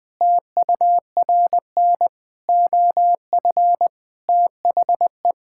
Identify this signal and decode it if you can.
Morse code, heard as a keyed tone, about 21 wpm, 700 Hz. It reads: TURN OF THE
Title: Sentences with 5-Word Phrases - 20wpm